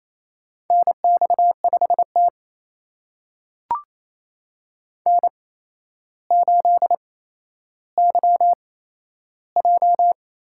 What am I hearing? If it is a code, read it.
Morse code, heard as a keyed tone, about 28 wpm, spaced as for 12 wpm, 700 Hz. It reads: NX5T N8YJ
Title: USA Call Signs - 4 Characters - 8x Character Spacing - 28wpm